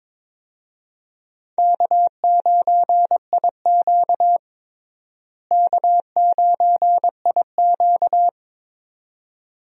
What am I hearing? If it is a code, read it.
Morse code, heard as a keyed tone, about 22 wpm, 700 Hz. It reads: K9IQ K9IQ
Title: USA Call Signs - 4 Characters - Repeated 2x - 22wpm